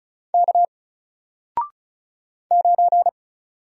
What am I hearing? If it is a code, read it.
Morse code, heard as a keyed tone, about 36 wpm, 700 Hz. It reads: K 9
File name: Single-Letter-Number-35wpm